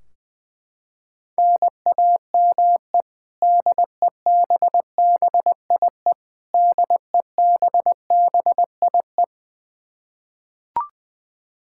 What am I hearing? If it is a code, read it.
Morse code, heard as a keyed tone, about 21 wpm, 700 Hz. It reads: NAME DEBBIE DEBBIE
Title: Practice QSO - Exchange 1 - Beginning - 20wpm